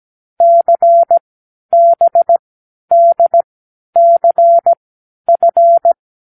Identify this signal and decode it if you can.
Morse code, heard as a keyed tone, about 17 wpm, spaced as for 10 wpm, 680 Hz. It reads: CBDCF
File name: k7qo / Lesson9(A-F)